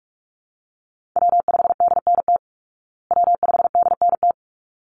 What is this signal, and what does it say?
Morse code, heard as a keyed tone, about 44 wpm, 700 Hz. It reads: W5DNT W5DNT
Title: Call Signs Repeated 3x - 45wpm